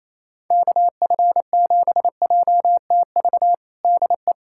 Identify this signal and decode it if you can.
Morse code, heard as a keyed tone, about 28 wpm, 700 Hz. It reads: KF7JTV DE
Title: USA Callsign DE Callsign 28wpm